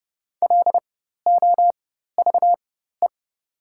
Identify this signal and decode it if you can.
Morse code, heard as a keyed tone, about 30 wpm, spaced as for 14 wpm, 700 Hz. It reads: LOVE
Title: 4x Character Spacing - Top 500 Words - 30wpm